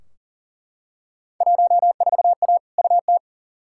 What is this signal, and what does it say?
Morse code, heard as a keyed tone, about 40 wpm, 700 Hz. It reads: 14A UT
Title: Field Day Exchanges Encore - 40wpm